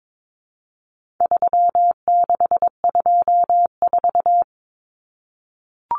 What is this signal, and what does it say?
Morse code, heard as a keyed tone, about 22 wpm, 700 Hz. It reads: 3624
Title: Common Cut Numbers - 4-Digit 22wpm